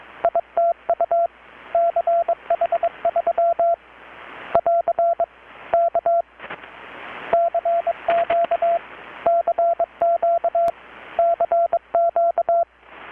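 Morse code, heard as a keyed tone, about 21 wpm, 680 Hz, with radio noise. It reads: ITU CH3 <AR> K CQ CQ CQ
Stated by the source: Morse from KPH's Night of Nights XX maritime radiotelegraphy event, received in Chiapas, Mexico about 3,800 km away on a Kenwood R-600 with a grounded 10 m long-wire antenna.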